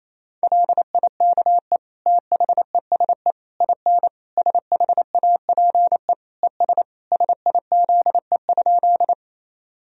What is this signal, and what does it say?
Morse code, heard as a keyed tone, about 28 wpm, 700 Hz. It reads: LIKE THESE IN SHAPE ES SIZE?
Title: Sentences from Top 500 Words and Abbreviations 28wpm